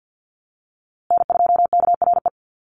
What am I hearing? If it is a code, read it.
Morse code, heard as a keyed tone, about 50 wpm, 700 Hz. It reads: N2KRE
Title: USA Call Signs Repeated 2x 50wpm